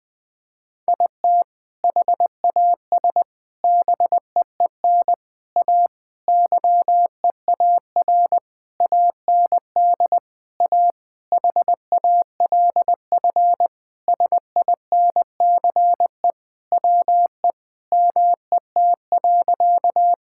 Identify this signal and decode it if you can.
Morse code, heard as a keyed tone, about 20 wpm, 700 Hz. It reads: IT HAS BEEN A YEAR AND A HALF SINCE WE MET.
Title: Sentences with 4-Word Phrases - 20wpm